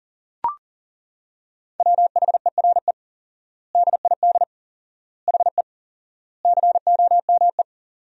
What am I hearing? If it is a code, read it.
Morse code, heard as a keyed tone, about 40 wpm, 700 Hz. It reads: WHERE DID HE COME
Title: 4x Word-Spacing - Sentences from Top 200 Words - 40wpm